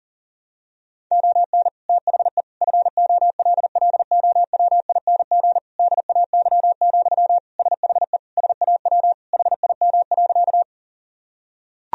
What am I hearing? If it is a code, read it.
Morse code, heard as a keyed tone, about 40 wpm, 700 Hz. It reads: ON THE FOLLOWING DAY, SHE SAW HIM.
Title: Sentences from Top 1000 Words - Review Part 4 - 40wpm